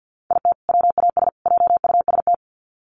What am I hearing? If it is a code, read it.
Morse code, heard as a keyed tone, about 50 wpm, 700 Hz. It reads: IT WAS JUST
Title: Sets of 3 Words 50wpm